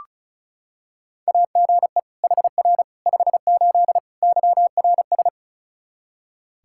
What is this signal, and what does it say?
Morse code, heard as a keyed tone, about 36 wpm, 700 Hz. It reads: AGE HR 58 YRS
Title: Practice QSO - Exchange 3 - More Info - 35wpm